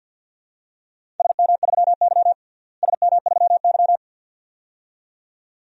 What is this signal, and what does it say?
Morse code, heard as a keyed tone, about 46 wpm, 700 Hz. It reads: SM3Y SM3Y
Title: DXCC-100 Call Signs - 4 Characters - Repeated 2x - 50wpm